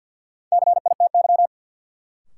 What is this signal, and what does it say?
Morse code, heard as a keyed tone, about 51 wpm, 700 Hz. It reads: CITY
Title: Sets of 2 Words - New Word Drill - 50wpm